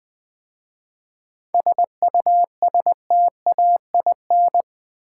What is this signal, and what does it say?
Morse code, heard as a keyed tone, about 20 wpm, 700 Hz. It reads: SUSTAIN
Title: 7 Letter Words - First 3 Letters Given - 20wpm